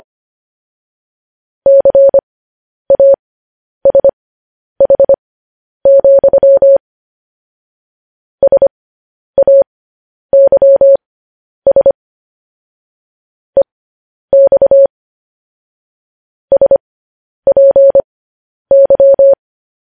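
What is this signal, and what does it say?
Morse code, heard as a keyed tone, about 25 wpm, spaced as for 10 wpm, 550 Hz. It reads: CASH, SAYS EX SPY
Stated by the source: Morse code news headlines, likely sent at 10 WPM.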